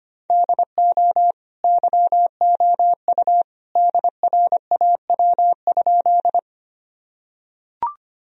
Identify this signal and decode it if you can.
Morse code, heard as a keyed tone, about 25 wpm, 700 Hz. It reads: DO YOU DRAW?
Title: Sentences from Top 1000 Words - Review Part 1 - 25wpm